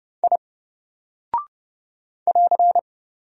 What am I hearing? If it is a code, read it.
Morse code, heard as a keyed tone, about 30 wpm, 700 Hz. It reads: I <AR>
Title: Single Character (Letter:Number:Punctuation:Prosign) 30wpm